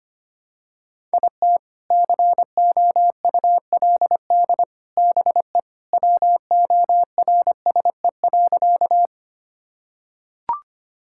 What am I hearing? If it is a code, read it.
Morse code, heard as a keyed tone, about 25 wpm, 700 Hz. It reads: IT COULD BE WORSE.
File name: practice / 25wpm / Sentences with 3-Word Phrases - 25wpm